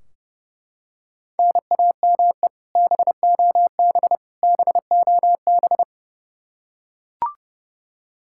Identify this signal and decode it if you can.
Morse code, heard as a keyed tone, about 30 wpm, 700 Hz. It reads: NAME BOB BOB
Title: Practice QSO - Exchange 1 - Beginning - 30wpm